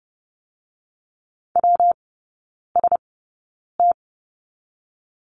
Morse code, heard as a keyed tone, about 30 wpm, 700 Hz. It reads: W S T
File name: Lesson 19e - Letter w - Three Characters 30wpm